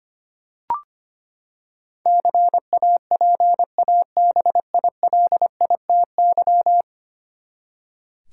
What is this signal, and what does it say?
Morse code, heard as a keyed tone, about 25 wpm, 700 Hz. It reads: CAPABILITY
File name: Ten Letter Words - 25wpm